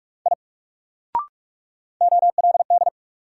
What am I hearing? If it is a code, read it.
Morse code, heard as a keyed tone, about 44 wpm, 700 Hz. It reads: I OLD
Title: Sentences from Top 200 Words - New Word Drill - 45wpm